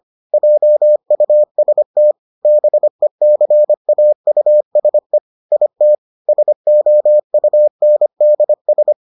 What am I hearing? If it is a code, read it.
Morse code, heard as a keyed tone, about 25 wpm, 600 Hz. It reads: JUST BECAUSE IT SOUNDS